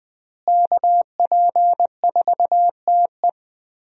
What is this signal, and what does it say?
Morse code, heard as a keyed tone, about 20 wpm, 700 Hz. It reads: KP4TE
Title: DXCC-100 Call Signs - 5 Characters - Repeated 2x - 20wpm